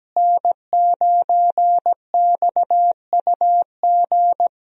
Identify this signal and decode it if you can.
Morse code, heard as a keyed tone, about 17 wpm, 700 Hz. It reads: N9XUG
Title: USA Call Signs - 5 Characters - Repeated 3x - 17wpm